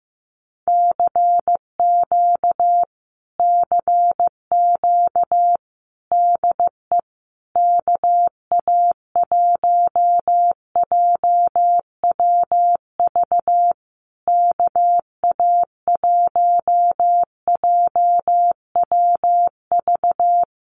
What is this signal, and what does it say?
Morse code, heard as a keyed tone, about 14 wpm, 700 Hz. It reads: CQ CQ DE KA1JWV KA1JWV
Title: Common QSO Phrases Encore 15wpm